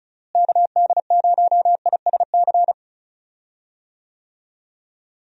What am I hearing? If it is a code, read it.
Morse code, heard as a keyed tone, about 36 wpm, 700 Hz. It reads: KD0ISC
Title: USA Call Signs - 6 Characters - Repeated 3x - 35wpm